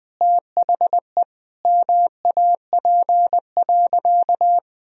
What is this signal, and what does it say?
Morse code, heard as a keyed tone, about 20 wpm, 700 Hz. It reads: THE MAP.
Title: Sentences from Top 1000 Words - Review Part 2 - 20wpm